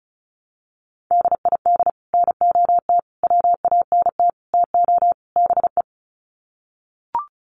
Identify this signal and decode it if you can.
Morse code, heard as a keyed tone, about 34 wpm, 700 Hz. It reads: DID NOT WANT TO BE
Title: Sets of 5 Words 35wpm